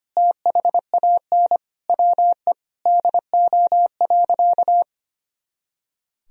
Morse code, heard as a keyed tone, about 25 wpm, 700 Hz. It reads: THAN WE DO.
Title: Sentences from Top 100 Words and Abbreviations 25wpm